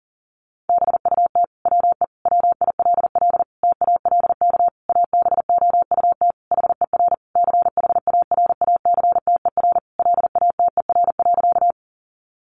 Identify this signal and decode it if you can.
Morse code, heard as a keyed tone, about 40 wpm, 700 Hz. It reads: BUT WE WILL TALK ABOUT HER CHARACTER LATER.
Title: Sentences from Top 1200 Words 40wpm